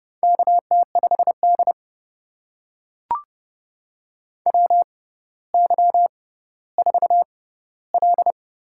Morse code, heard as a keyed tone, about 30 wpm, spaced as for 15 wpm, 700 Hz. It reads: KT5D WY4L
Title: USA Call Signs - 6x Character Spacing - 30wpm